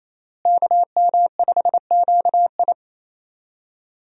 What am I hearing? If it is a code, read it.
Morse code, heard as a keyed tone, about 28 wpm, 700 Hz. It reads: KM5QI
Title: USA Call Signs - 5 Characters - Repeated 3x - 28wpm